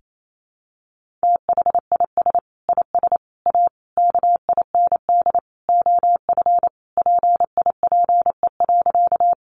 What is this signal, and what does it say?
Morse code, heard as a keyed tone, about 28 wpm, 700 Hz. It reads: THIS IS A KIND OF PIPE.
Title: Sentences from Top 1500 Words 28wpm